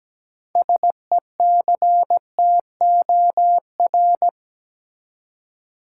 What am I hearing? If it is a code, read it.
Morse code, heard as a keyed tone, about 17 wpm, 700 Hz. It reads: SECTOR
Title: Six Letter Words - 17wpm